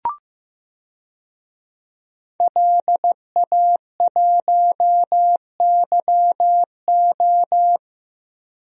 Morse code, heard as a keyed tone, about 15 wpm, 700 Hz. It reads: LA1YO
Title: DXCC-100 Call Signs - 5 Characters - Repeated 3x - 15wpm